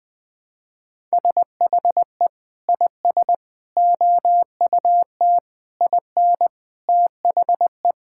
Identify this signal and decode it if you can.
Morse code, heard as a keyed tone, about 20 wpm, 700 Hz. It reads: SHE IS OUT IN THE
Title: Sentences with 3-Word Phrases - 20wpm